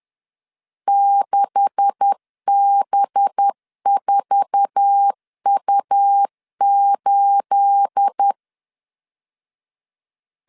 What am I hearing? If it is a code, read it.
Morse code, heard as a keyed tone, about 10 wpm, 800 Hz. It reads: 6B4U8
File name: CW / DET / 1 9 1500C S A 10wpm 4511